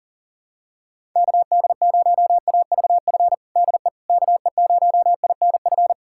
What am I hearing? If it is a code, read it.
Morse code, heard as a keyed tone, about 40 wpm, 700 Hz. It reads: KD0AVF DE KE0INF
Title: USA Callsign DE Callsign 40wpm